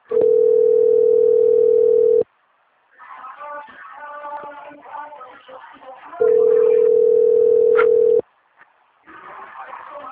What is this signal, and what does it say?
Ringback tone (North America), 440 + 480 Hz.
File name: rg-1600-unknown- - - .15039